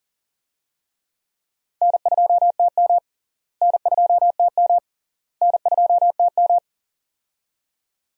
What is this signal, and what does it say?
Morse code, heard as a keyed tone, about 40 wpm, 700 Hz. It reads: N2TM N2TM N2TM
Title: USA Call Signs - 4 Characters - Repeated 3x - 40wpm